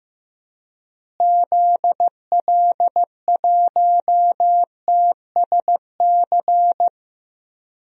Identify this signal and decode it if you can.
Morse code, heard as a keyed tone, about 14 wpm, 700 Hz. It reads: ZL1TSC
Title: DXCC-100 Call Signs - 6 Characters - Repeated 3x - 15wpm